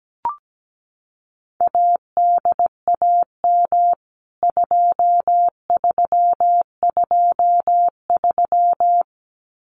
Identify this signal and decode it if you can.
Morse code, heard as a keyed tone, about 17 wpm, 700 Hz. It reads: ADAM 2323
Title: CWT Exchange 17wpm